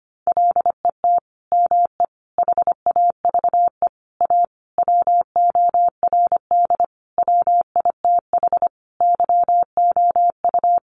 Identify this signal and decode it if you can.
Morse code, heard as a keyed tone, about 25 wpm, 700 Hz. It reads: LET ME HAVE A WORD WITH YOU
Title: Sentences from Top 900 Words 25wpm